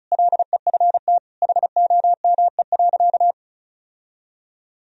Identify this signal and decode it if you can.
Morse code, heard as a keyed tone, about 36 wpm, 700 Hz. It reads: LEFT HOME.
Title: Sentences from Top 1000 Words - Review Part 4 - 35wpm